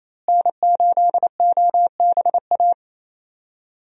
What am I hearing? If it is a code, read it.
Morse code, heard as a keyed tone, about 28 wpm, 700 Hz. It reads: N8OBA
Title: USA Call Signs - 5 Characters - Repeated 3x - 28wpm